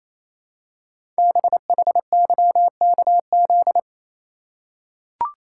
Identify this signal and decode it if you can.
Morse code, heard as a keyed tone, about 27 wpm, 700 Hz. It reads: BHYKZ
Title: Code Groups - 5 Letters - 28wpm